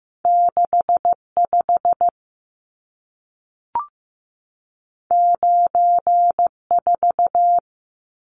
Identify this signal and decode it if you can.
Morse code, heard as a keyed tone, about 15 wpm, 700 Hz. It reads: 65 94
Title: Numbers - 2-Digit 15wpm